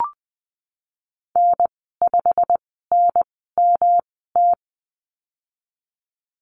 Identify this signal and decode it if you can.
Morse code, heard as a keyed tone, about 19 wpm, spaced as for 14 wpm, 700 Hz. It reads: N5NMT
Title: 147 Lesson 24g - Letter y - Call Signs 20wpm